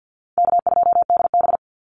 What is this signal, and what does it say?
Morse code, heard as a keyed tone, about 50 wpm, 700 Hz. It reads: K2DB
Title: Easy Call Signs - 50wpm